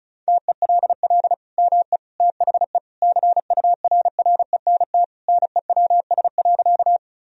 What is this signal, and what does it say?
Morse code, heard as a keyed tone, about 36 wpm, 700 Hz. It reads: TELL ME THE CURRENT NEWS.
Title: Sentences from Top 1000 Words - Review Part 4 - 35wpm